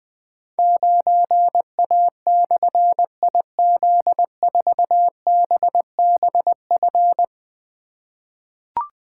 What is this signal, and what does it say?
Morse code, heard as a keyed tone, about 20 wpm, 700 Hz. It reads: 9A/IZ4BBF
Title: Difficult Call Signs 20wpm